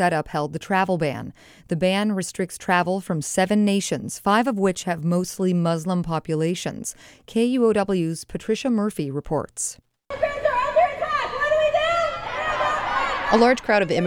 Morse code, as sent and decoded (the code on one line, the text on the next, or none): none